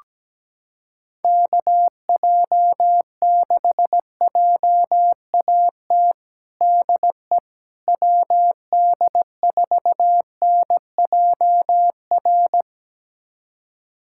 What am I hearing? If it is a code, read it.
Morse code, heard as a keyed tone, about 17 wpm, 700 Hz. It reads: KJ6JAT DE WD4NJR
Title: USA Callsign DE Callsign - 6 Characters - 17wpm